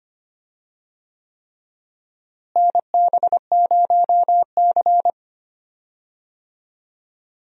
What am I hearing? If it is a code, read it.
Morse code, heard as a keyed tone, about 25 wpm, 700 Hz. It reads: NB0C
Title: USA Call Signs - 4 Characters - Repeated 3x - 25wpm